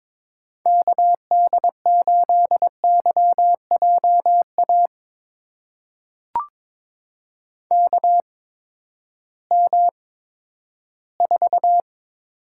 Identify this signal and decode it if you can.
Morse code, heard as a keyed tone, about 21 wpm, spaced as for 12 wpm, 700 Hz. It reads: KD8YJA KM4
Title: USA Call Signs - 6 Characters - 8x Character Spacing - 22wpm